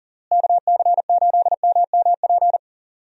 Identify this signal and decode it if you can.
Morse code, heard as a keyed tone, about 40 wpm, 700 Hz. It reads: KC8MMP
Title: USA Call Signs - 6 Characters - Repeated 3x - 40wpm